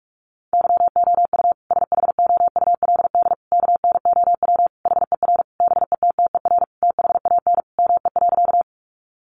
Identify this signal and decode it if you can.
Morse code, heard as a keyed tone, about 44 wpm, 700 Hz. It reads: YOU SHOULD KNOW HER BETTER THAN ME.
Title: Sentences from Top 400 Words 45wpm